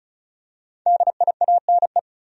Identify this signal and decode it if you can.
Morse code, heard as a keyed tone, about 36 wpm, 700 Hz. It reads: DIANE
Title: USA Common Names - Max 5 Letters - 35wpm